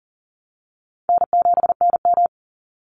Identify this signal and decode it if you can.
Morse code, heard as a keyed tone, about 40 wpm, 700 Hz. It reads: N7NM
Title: Call Signs Repeated 3x - 40wpm